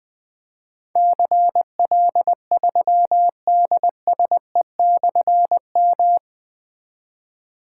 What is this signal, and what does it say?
Morse code, heard as a keyed tone, about 21 wpm, 700 Hz. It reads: CL3DSE/M
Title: Difficult Call Signs 20wpm